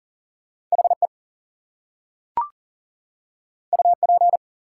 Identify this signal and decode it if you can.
Morse code, heard as a keyed tone, about 40 wpm, 700 Hz. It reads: HE UP
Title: Two Letter Words - 40wpm